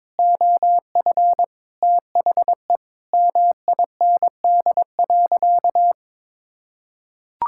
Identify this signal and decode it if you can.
Morse code, heard as a keyed tone, about 21 wpm, 700 Hz. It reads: OF THE MIND.
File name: practice / 22wpm / Sentences with 4-Word Phrases - 22wpm